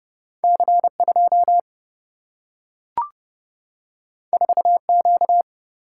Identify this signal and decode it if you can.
Morse code, heard as a keyed tone, about 30 wpm, 700 Hz. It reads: C2 4Q
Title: Single-Letter Single-Number 30wpm